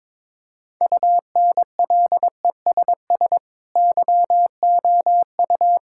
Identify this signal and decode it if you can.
Morse code, heard as a keyed tone, about 22 wpm, 700 Hz. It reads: UNLESS YOU
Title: Sentences from Top 1000 Words - Review Part 5 - 22wpm